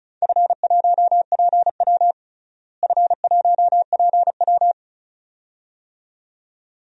Morse code, heard as a keyed tone, about 35 wpm, 690 Hz. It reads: F1PW F1PW
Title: DXCC-100 Call Signs - 4 Characters - Repeated 2x - 35wpm